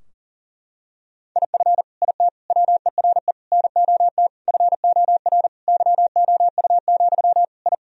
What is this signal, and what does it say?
Morse code, heard as a keyed tone, about 40 wpm, 700 Hz. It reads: IF IT WERE NOT FOR YOU, I
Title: Sentences from Top 1000 Words - Review Part 2 - 40wpm